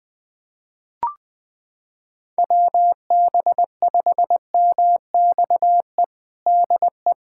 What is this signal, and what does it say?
Morse code, heard as a keyed tone, about 20 wpm, 700 Hz. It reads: WB5MXE DE
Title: USA Callsign DE Callsign - 6 Characters - 20wpm